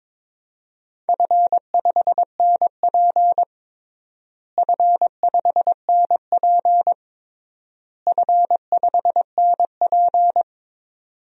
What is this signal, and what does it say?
Morse code, heard as a keyed tone, about 22 wpm, 700 Hz. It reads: F5NP F5NP F5NP
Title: DXCC-100 Call Signs - 4 Characters - Repeated 3x - 22wpm